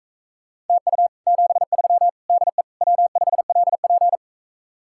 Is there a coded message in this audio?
Yes: Morse code from 0.7 to 4.2 s, heard as a keyed tone, about 45 wpm, 700 Hz. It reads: TU 73 DE W5LP